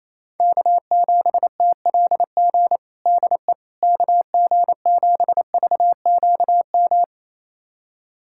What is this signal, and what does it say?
Morse code, heard as a keyed tone, about 28 wpm, 700 Hz. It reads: K7TLG DE KG7VQM
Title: USA Callsign DE Callsign 28wpm